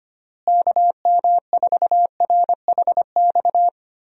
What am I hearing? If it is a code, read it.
Morse code, heard as a keyed tone, about 25 wpm, 700 Hz. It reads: KM4RHX
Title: USA Call Signs - 6 Characters - Repeated 3x - 25wpm